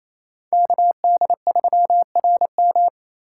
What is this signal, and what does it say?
Morse code, heard as a keyed tone, about 36 wpm, 700 Hz. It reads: KD3RM